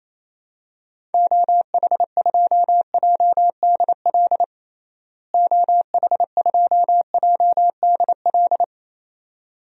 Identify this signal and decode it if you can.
Morse code, heard as a keyed tone, about 28 wpm, 700 Hz. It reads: OH2JDL OH2JDL
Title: DXCC-100 Call Signs - 6 Characters - Repeated 2x - 28wpm